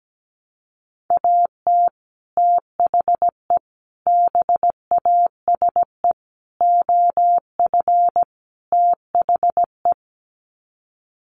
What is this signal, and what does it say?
Morse code, heard as a keyed tone, about 17 wpm, 700 Hz. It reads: AT THE BASE OF THE
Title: Sets of 5 Words 17wpm